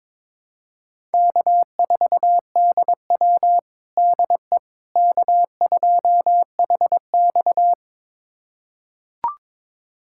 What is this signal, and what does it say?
Morse code, heard as a keyed tone, about 22 wpm, 700 Hz. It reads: K4DW DE K2HX
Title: USA Callsign DE Callsign - 4 Characters - 22wpm